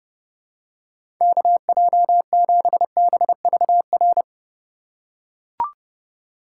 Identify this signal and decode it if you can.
Morse code, heard as a keyed tone, about 30 wpm, 700 Hz. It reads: KJ7BVR
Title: USA Call Signs - 6 Characters - 3x Character Spacing - 30wpm